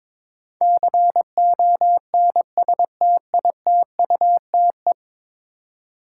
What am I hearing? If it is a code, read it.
Morse code, heard as a keyed tone, about 23 wpm, 700 Hz. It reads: CONSTITUTE
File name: practice / 22wpm / 10 Letter Words - First 3 Letters Given - 22wpm